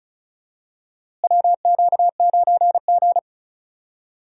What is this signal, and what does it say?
Morse code, heard as a keyed tone, about 34 wpm, 700 Hz. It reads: WQ9G
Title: USA Call Signs - 4 Characters - Repeated 3x - 35wpm